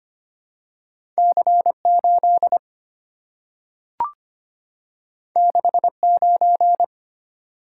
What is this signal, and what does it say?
Morse code, heard as a keyed tone, about 25 wpm, 700 Hz. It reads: C8 69
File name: Two Letter-Number Combinations - 25wpm